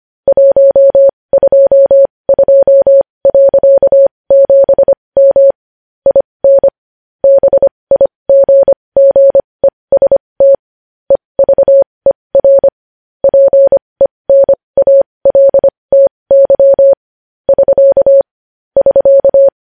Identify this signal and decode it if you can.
Morse code, heard as a keyed tone, about 25 wpm, spaced as for 20 wpm, 550 Hz. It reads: 122.7M IN BIGGEST EVER PENALTY <SK> <SK>
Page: news-headlines-in-morse-code-at-20-wpm